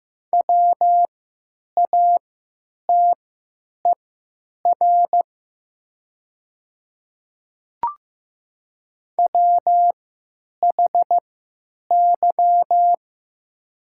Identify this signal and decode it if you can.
Morse code, heard as a keyed tone, about 15 wpm, spaced as for 9 wpm, 700 Hz. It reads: WATER WHY
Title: 3x Character Spacing - Top 200 Words - 15wpm